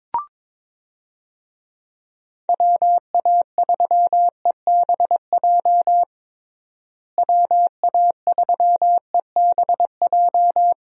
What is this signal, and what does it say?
Morse code, heard as a keyed tone, about 21 wpm, 700 Hz. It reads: WA3EBJ WA3EBJ
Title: USA Call Signs - 6 Characters - Repeated 3x - 22wpm